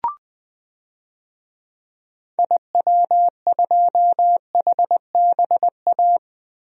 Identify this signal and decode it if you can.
Morse code, heard as a keyed tone, about 20 wpm, 700 Hz. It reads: IW2HBA
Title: DXCC-100 Call Signs - 6 Characters - Repeated 3x - 20wpm